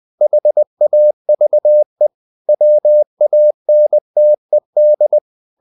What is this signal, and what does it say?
Morse code, heard as a keyed tone, about 20 wpm, 600 Hz. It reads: HAVE WANTED
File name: the old man and the sea part 2 20wpm